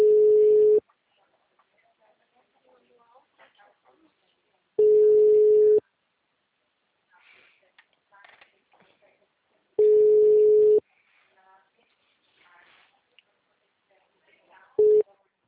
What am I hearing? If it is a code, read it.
Ringback tone (Europe, 425 Hz), with speech.
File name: out- -2008- - - .11216